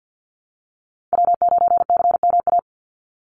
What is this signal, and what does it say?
Morse code, heard as a keyed tone, about 50 wpm, 700 Hz. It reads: W9CMA
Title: USA Call Signs 50wpm